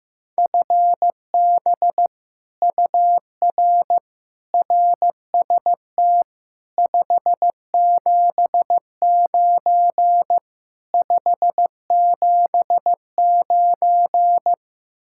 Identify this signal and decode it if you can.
Morse code, heard as a keyed tone, about 15 wpm, 700 Hz. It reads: FB UR RST 579 579
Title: Speed-Racing Common QSO Phrases Encore - 15wpm